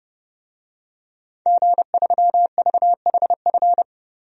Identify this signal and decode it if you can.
Morse code, heard as a keyed tone, about 30 wpm, 700 Hz. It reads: G3VHF
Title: DXCC-100 Call Signs - 5 Characters - 30wpm